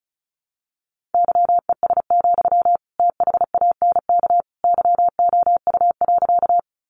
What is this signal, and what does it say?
Morse code, heard as a keyed tone, about 34 wpm, 700 Hz. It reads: YES, THANK YOU.
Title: Sentences from Top 600 Words 35wpm